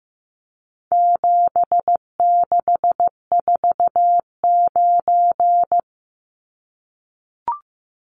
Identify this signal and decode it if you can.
Morse code, heard as a keyed tone, about 15 wpm, 700 Hz. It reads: 7649
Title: Numbers - 4-Digit 15wpm